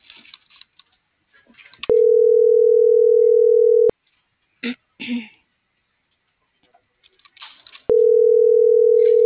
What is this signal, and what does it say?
Ringback tone (North America), 440 + 480 Hz.